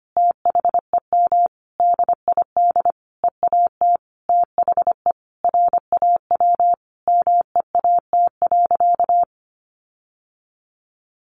Morse code, heard as a keyed tone, about 25 wpm, 700 Hz. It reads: THEM DID EAT THE RAW MEAT.
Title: Sentences from Top 1500 Words 25wpm